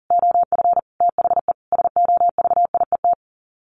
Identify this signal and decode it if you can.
Morse code, heard as a keyed tone, about 40 wpm, 700 Hz. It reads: OF THE SOVIET
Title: Sets of 5 Words Encore 40wpm